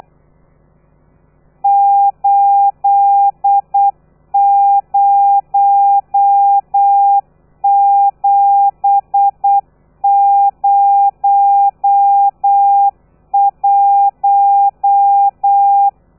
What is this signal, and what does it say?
Morse code, heard as a keyed tone, about 8 wpm, 800 Hz. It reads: 80701
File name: telegrafia / i2tfj / Lez13 35